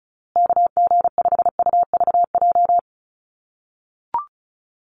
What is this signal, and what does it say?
Morse code, heard as a keyed tone, about 34 wpm, 700 Hz. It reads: KG5UVJ
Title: USA Call Signs Repeated 3x 35wpm